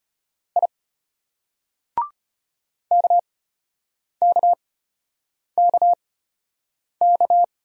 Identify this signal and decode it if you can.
Morse code, heard as a keyed tone, about 33 wpm, spaced as for 8 wpm, 700 Hz. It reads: I KKKK